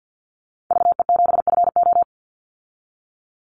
Morse code, heard as a keyed tone, about 50 wpm, 700 Hz. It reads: VE7FO
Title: Call Signs Repeated 3x - 50wpm